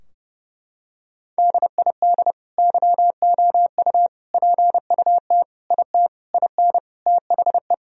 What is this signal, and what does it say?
Morse code, heard as a keyed tone, about 30 wpm, 700 Hz. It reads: DID YOU PUT IT IN THE
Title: Sentences from Top 1000 Words - Review Part 4 - 30wpm